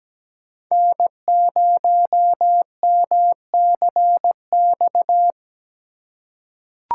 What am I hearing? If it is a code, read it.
Morse code, heard as a keyed tone, about 17 wpm, 700 Hz. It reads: N0MCX
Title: USA Call Signs - 5 Characters - 2x Character Spacing - 17wpm